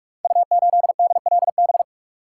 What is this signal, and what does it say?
Morse code, heard as a keyed tone, about 44 wpm, 700 Hz. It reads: U8DLB
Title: Code Groups - 5 Letter:Numbers - 45wpm